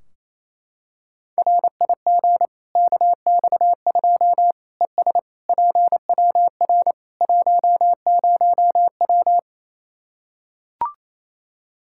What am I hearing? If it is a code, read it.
Morse code, heard as a keyed tone, about 28 wpm, 700 Hz. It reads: RIG KX2 ES PWR 10W
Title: K1USN SST QSO Practice 28wpm